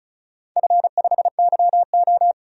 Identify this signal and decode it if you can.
Morse code, heard as a keyed tone, about 36 wpm, 700 Hz. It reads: F5YO
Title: DXCC-100 Call Signs - 4 Characters - 35wpm